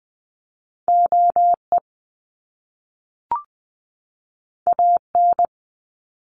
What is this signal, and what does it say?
Morse code, heard as a keyed tone, about 20 wpm, 700 Hz. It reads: OE AN
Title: Top Two Letter Combinations in Words 20wpm